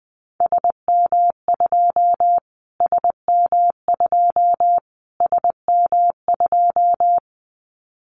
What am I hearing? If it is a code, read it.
Morse code, heard as a keyed tone, about 20 wpm, 700 Hz. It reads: SM2 SM2 SM2
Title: Callsign Prefixes for Top 25 DXCC - Repeated 3x - 20wpm